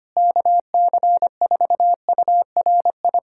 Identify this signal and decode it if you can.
Morse code, heard as a keyed tone, about 25 wpm, 700 Hz. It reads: KC4URI
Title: USA Callsign DE Callsign 25wpm